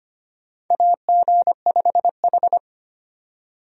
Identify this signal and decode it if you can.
Morse code, heard as a keyed tone, about 25 wpm, 700 Hz. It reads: AG5H
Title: USA Call Signs - 4 Characters - 25wpm